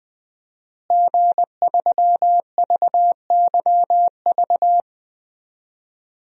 Morse code, heard as a keyed tone, about 20 wpm, 700 Hz. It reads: G3VYV
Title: DXCC-100 Call Signs - 5 Characters - Repeated 3x - 20wpm